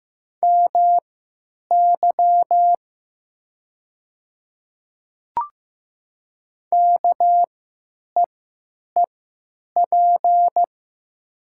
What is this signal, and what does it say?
Morse code, heard as a keyed tone, about 15 wpm, spaced as for 9 wpm, 700 Hz. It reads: MY KEEP
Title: 3x Character Spacing - Top 500 Words - 15wpm